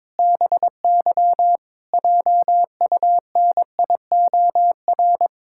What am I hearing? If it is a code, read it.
Morse code, heard as a keyed tone, about 22 wpm, 700 Hz. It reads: BY JUNIOR
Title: Pangram Challenge v2 - 22wpm